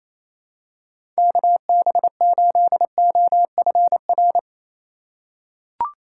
Morse code, heard as a keyed tone, about 28 wpm, 700 Hz. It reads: KB8OFR
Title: USA Call Signs - 6 Characters - 3x Character Spacing - 28wpm